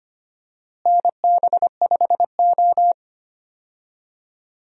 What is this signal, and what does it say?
Morse code, heard as a keyed tone, about 25 wpm, 700 Hz. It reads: NB5O
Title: USA Call Signs - 4 Characters - Repeated 3x - 25wpm